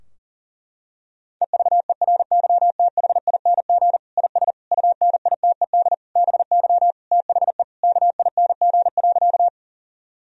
Morse code, heard as a keyed tone, about 40 wpm, 700 Hz. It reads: EVERYTHING IS UNITED BY THE KING.
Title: Sentences from Top 1000 Words - Review Part 4 - 40wpm